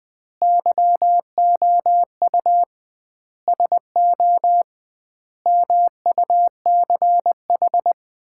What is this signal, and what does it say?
Morse code, heard as a keyed tone, about 20 wpm, 700 Hz. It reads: YOU SO MUCH
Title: 2x Word-Spacing - Sets of 4 Words - 20wpm